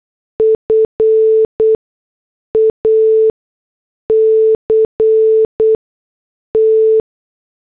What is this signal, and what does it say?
Morse code, heard as a keyed tone, about 8 wpm, spaced as for 6 wpm, 440 Hz. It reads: FACT
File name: morse / headlines